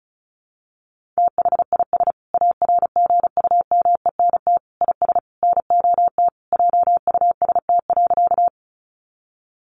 Morse code, heard as a keyed tone, about 34 wpm, 700 Hz. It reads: THIS ARGUMENT IS NOT JUST.
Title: Sentences from Top 1400 Words 35wpm